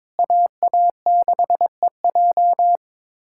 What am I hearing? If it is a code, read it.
Morse code, heard as a keyed tone, about 22 wpm, 700 Hz. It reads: AA6EJ
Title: USA Callsign DE Callsign - 5 Characters - 22wpm